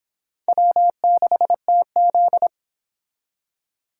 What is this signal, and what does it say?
Morse code, heard as a keyed tone, about 25 wpm, 700 Hz. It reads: W6TZ